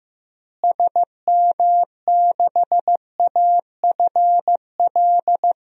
Morse code, heard as a keyed tone, about 16 wpm, 700 Hz. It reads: SM6AFL
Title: DXCC-100 Call Signs - 6 Characters - Repeated 3x - 15wpm